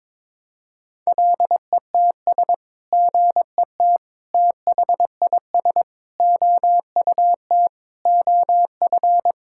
Morse code, heard as a keyed tone, about 21 wpm, 700 Hz. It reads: LETS GET THIS OUT OF
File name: practice / 22wpm / Sets of 7 Words - 22wpm